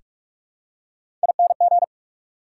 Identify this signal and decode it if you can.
Morse code, heard as a keyed tone, about 42 wpm, 700 Hz. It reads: ING
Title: Prefixes and Suffixes 45wpm